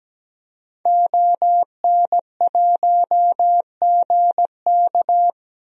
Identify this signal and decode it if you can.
Morse code, heard as a keyed tone, about 17 wpm, 700 Hz. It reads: ON1GK
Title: DXCC-100 Call Signs - 5 Characters - Repeated 2x - 17wpm